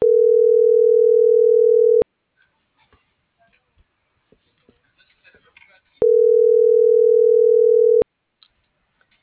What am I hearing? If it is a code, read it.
Ringback tone (North America), 440 + 480 Hz.